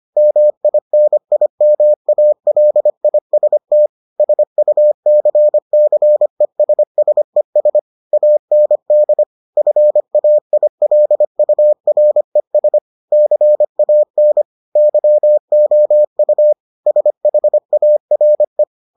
Morse code, heard as a keyed tone, about 25 wpm, 600 Hz. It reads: MINIMALIST SUCCESSES AND FAILURES CAN YOU SHARE